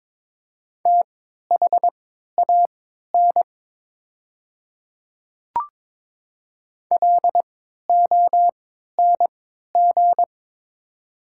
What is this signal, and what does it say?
Morse code, heard as a keyed tone, about 22 wpm, spaced as for 12 wpm, 700 Hz. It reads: THAN LONG
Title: 3x Character Spacing - Top 300 Words - 22wpm